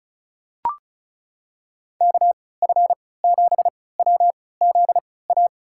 Morse code, heard as a keyed tone, about 36 wpm, spaced as for 20 wpm, 700 Hz. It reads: KF7WZA
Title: USA Call Signs - 6 Characters - 3x Character Spacing - 35wpm